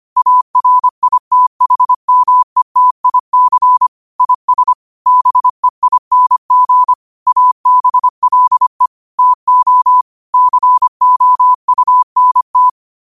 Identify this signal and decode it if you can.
Morse code, heard as a keyed tone, about 25 wpm, 1000 Hz. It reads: ARITHMETIC IS BEING ABLE TO COUNT
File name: qotdfiles / 25wpm